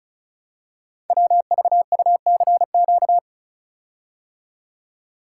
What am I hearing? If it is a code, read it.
Morse code, heard as a keyed tone, about 33 wpm, 700 Hz. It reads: WVUCQ
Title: Code Groups - 5 Letters - 35wpm